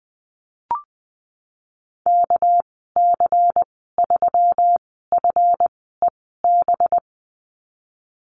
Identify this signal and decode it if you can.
Morse code, heard as a keyed tone, about 20 wpm, spaced as for 15 wpm, 700 Hz. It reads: KC3FEB
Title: 200 Lesson 32g - Letter k - Call Signs 20wpm